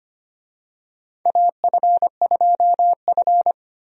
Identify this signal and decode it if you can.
Morse code, heard as a keyed tone, about 25 wpm, 700 Hz. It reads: AF2F
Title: USA Call Signs - 4 Characters - Repeated 3x - 25wpm